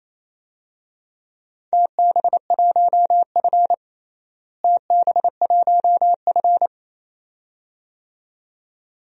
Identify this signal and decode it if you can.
Morse code, heard as a keyed tone, about 27 wpm, 700 Hz. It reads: TB1F TB1F
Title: DXCC-100 Call Signs - 4 Characters - Repeated 2x - 28wpm